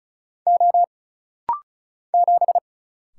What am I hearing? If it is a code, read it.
Morse code, heard as a keyed tone, about 36 wpm, 700 Hz. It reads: O 7
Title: Single Letter-Number - Rapid-Fire 35wpm